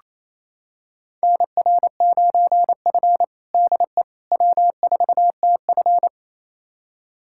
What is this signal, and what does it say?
Morse code, heard as a keyed tone, about 28 wpm, 700 Hz. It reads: NR9F DE W4TF
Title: USA Callsign DE Callsign - 4 Characters - 28wpm